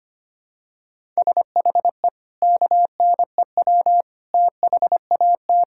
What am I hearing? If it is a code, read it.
Morse code, heard as a keyed tone, about 25 wpm, 700 Hz. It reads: SHE KNEW THAT
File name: practice / 25wpm / Sentences with 4-Word Phrases - 25wpm